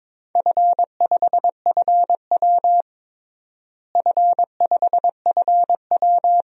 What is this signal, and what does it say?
Morse code, heard as a keyed tone, about 22 wpm, 700 Hz. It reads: F5FW F5FW
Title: DXCC-100 Call Signs - 4 Characters - Repeated 3x - 22wpm